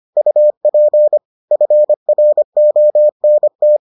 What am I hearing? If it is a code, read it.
Morse code, heard as a keyed tone, about 25 wpm, 600 Hz. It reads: UP FRONT